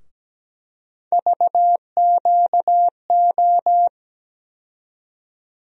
Morse code, heard as a keyed tone, about 17 wpm, 700 Hz. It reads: VQO
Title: USA Call Sign Postfixes - 3 Letters 17wpm